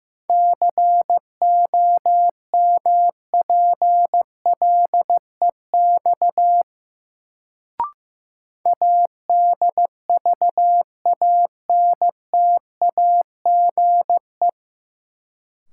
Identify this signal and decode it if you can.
Morse code, heard as a keyed tone, about 15 wpm, 700 Hz. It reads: COMPLEX ADVANTAGE
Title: Sentences from Top 900 Words - New Word Drill - 15wpm